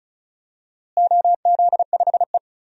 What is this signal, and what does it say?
Morse code, heard as a keyed tone, about 34 wpm, 700 Hz. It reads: OZ5E
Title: DXCC-100 Call Signs - 4 Characters - Repeated 3x - 35wpm